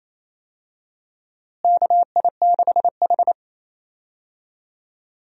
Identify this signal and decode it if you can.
Morse code, heard as a keyed tone, about 28 wpm, 700 Hz. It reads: KI6H
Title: USA Call Signs - 4 Characters - Repeated 3x - 28wpm